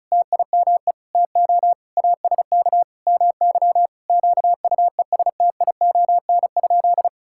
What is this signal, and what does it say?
Morse code, heard as a keyed tone, about 36 wpm, 700 Hz. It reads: TIME TO ASK MY QUESTION?
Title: Sentences from Top 1000 Words - Review Part 5 - 35wpm